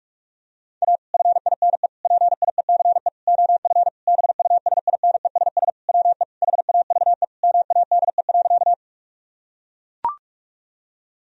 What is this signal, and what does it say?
Morse code, heard as a keyed tone, about 41 wpm, 700 Hz. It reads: A FINE PIECE OF BUSINESS WE HAVE MADE.
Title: Sentences from Top 1000 Words - Review Part 2 - 45wpm